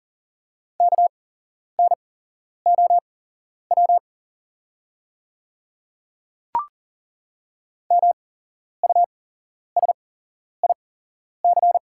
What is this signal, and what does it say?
Morse code, heard as a keyed tone, about 40 wpm, spaced as for 11 wpm, 690 Hz. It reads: KNOW MUSIC
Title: 8x Character Spacing - Top 500 Words - 40wpm